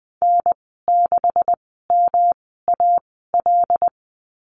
Morse code, heard as a keyed tone, about 20 wpm, spaced as for 15 wpm, 700 Hz. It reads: N6MAL
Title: 186 Lesson 30g - Letter b - Call Signs 20wpm